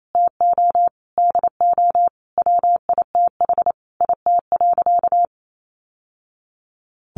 Morse code, heard as a keyed tone, about 28 wpm, 700 Hz. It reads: TO DO WITH IT.
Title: Sentences from Top 1100 Words 28wpm